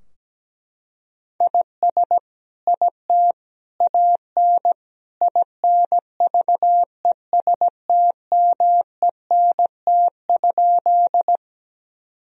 Morse code, heard as a keyed tone, about 17 wpm, 700 Hz. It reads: IS IT AN INVESTMENT?
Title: Sentences from Top 1000 Words - Review Part 4 - 17wpm